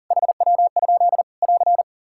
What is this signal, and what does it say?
Morse code, heard as a keyed tone, about 40 wpm, 700 Hz. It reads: HW? <AR>
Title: K1USN SST QSO Practice 40wpm